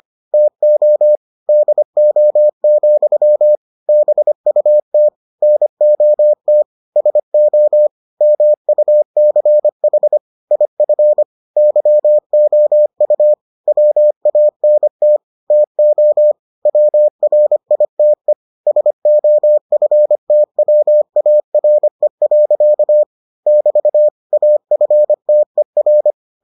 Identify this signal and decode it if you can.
Morse code, heard as a keyed tone, about 25 wpm, 600 Hz. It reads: TO DO, BUT NOT SO MUCH IF YOU WANT TO WRITE SOFTWARE. <BT> AFTER